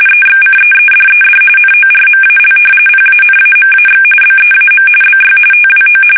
RTTY, about 45.45 baud. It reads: EXCEEDED NORMAL PANAMAX BEAM BY TWO
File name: fsk850 / battleships